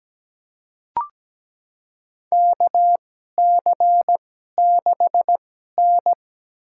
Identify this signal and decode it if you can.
Morse code, heard as a keyed tone, about 17 wpm, spaced as for 12 wpm, 700 Hz. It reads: KC6N
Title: USA Call Signs - 4 Characters - 2x Character Spacing - 17wpm